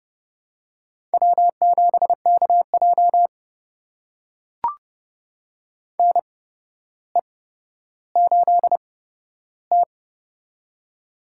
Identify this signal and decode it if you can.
Morse code, heard as a keyed tone, about 30 wpm, spaced as for 13 wpm, 700 Hz. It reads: W7KJ NE8T
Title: USA Call Signs - 4 Characters - 8x Character Spacing - 30wpm